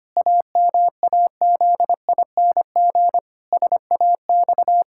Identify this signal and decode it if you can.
Morse code, heard as a keyed tone, about 25 wpm, 700 Hz. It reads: AMAZING SAX
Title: Pangram Challenge v2 - 25wpm